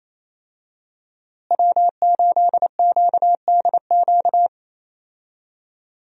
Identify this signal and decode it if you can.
Morse code, heard as a keyed tone, about 28 wpm, 700 Hz. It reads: W8QDQ
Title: USA Call Signs - 5 Characters - Repeated 2x - 28wpm